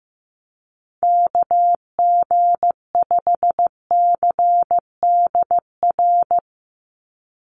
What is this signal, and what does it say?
Morse code, heard as a keyed tone, about 15 wpm, 700 Hz. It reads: KG5CDR
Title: USA Call Signs 15wpm